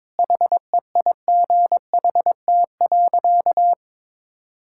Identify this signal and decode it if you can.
Morse code, heard as a keyed tone, about 21 wpm, 700 Hz. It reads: HEIGHT.
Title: Sentences from Top 1600 Words 22wpm